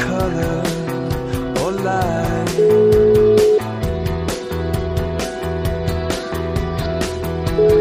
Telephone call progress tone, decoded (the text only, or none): ringback tone